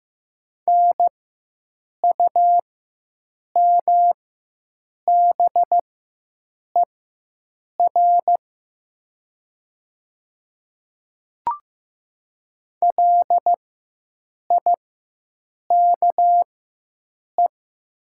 Morse code, heard as a keyed tone, about 15 wpm, spaced as for 7 wpm, 700 Hz. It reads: NUMBER LIKE
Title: 4x Character Spacing - Top 200 Words - 15wpm